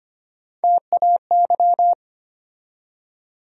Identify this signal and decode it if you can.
Morse code, heard as a keyed tone, about 25 wpm, 700 Hz. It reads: TAY
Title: USA Call Sign Postfixes - 3 Letters 25wpm